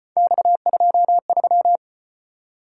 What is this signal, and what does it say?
Morse code, heard as a keyed tone, about 33 wpm, 700 Hz. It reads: X23